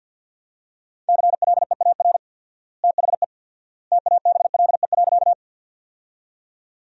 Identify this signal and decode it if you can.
Morse code, heard as a keyed tone, about 50 wpm, 700 Hz. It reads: CLEAR THE TABLE.
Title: 4x Word-Spacing - Sentences from Top 500 Words - 50wpm